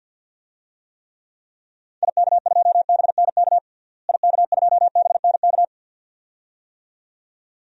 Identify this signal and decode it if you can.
Morse code, heard as a keyed tone, about 52 wpm, 700 Hz. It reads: IK2BNK IK2BNK
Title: DXCC-100 Call Signs - 6 Characters - Repeated 2x - 50wpm